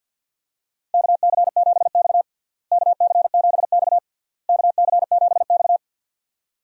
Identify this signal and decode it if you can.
Morse code, heard as a keyed tone, about 45 wpm, 700 Hz. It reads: KC7X KC7X KC7X
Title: USA Call Signs - 4 Characters - Repeated 3x - 50wpm